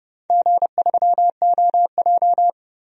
Morse code, heard as a keyed tone, about 29 wpm, 700 Hz. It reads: G3OJ
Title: DXCC-100 Call Signs - 4 Characters - 30wpm